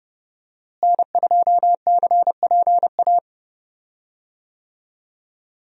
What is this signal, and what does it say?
Morse code, heard as a keyed tone, about 30 wpm, 700 Hz. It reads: N2CPA
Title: USA Call Signs - 5 Characters - Repeated 3x - 30wpm